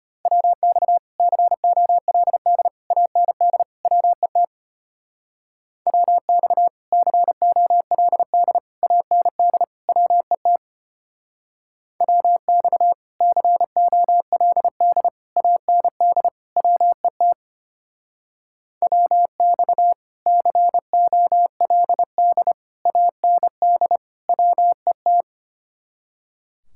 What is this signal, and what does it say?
Morse code, heard as a keyed tone, about 33 wpm, 700 Hz. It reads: WX COLD AND WET WX COLD AND WET WX COLD AND WET WX COLD AND WET